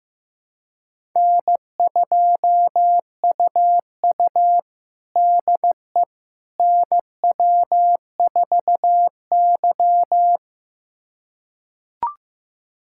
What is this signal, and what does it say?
Morse code, heard as a keyed tone, about 15 wpm, 700 Hz. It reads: N2UU DE NW4Y
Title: USA Callsign DE Callsign - 4 Characters - 15wpm